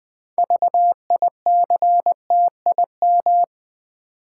Morse code, heard as a keyed tone, about 20 wpm, 700 Hz. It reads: VICTIM
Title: 6 Letter Words - First 3 Letters Given - 20wpm